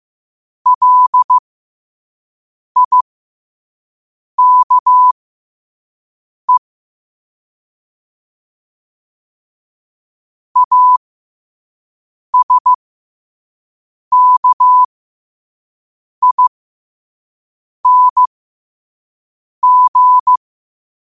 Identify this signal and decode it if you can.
Morse code, heard as a keyed tone, about 15 wpm, spaced as for 5 wpm, 1000 Hz. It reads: LIKE ASKING
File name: qotdfiles / 5wpm